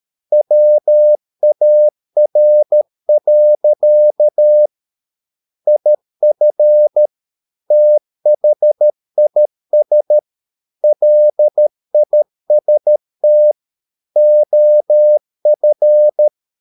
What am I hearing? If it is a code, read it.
Morse code, heard as a keyed tone, about 13 wpm, 600 Hz. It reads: WAR. IF THIS LIST OF